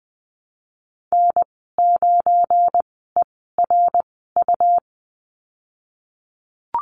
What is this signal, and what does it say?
Morse code, heard as a keyed tone, about 20 wpm, spaced as for 14 wpm, 700 Hz. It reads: N9ERU
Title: 245 Lesson 39g - Letter z - Call Signs 20wpm